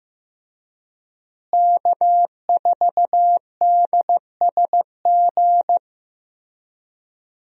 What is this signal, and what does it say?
Morse code, heard as a keyed tone, about 15 wpm, 700 Hz. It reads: K4DSG
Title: USA Call Signs - 5 Characters - Repeated 2x - 15wpm